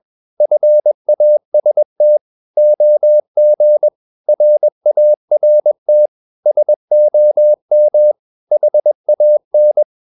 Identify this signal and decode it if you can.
Morse code, heard as a keyed tone, about 21 wpm, 600 Hz. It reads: FAST OG RART SOM HAN
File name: Ferie QRS 21wpm